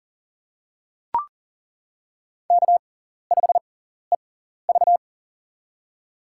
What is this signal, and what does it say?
Morse code, heard as a keyed tone, about 40 wpm, spaced as for 14 wpm, 700 Hz. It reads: K5EV
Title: USA Call Signs - 4 Characters - 6x Character Spacing - 40wpm